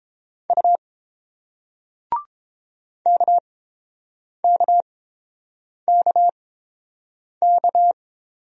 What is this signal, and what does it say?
Morse code, heard as a keyed tone, about 31 wpm, spaced as for 7 wpm, 690 Hz. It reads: U KKKK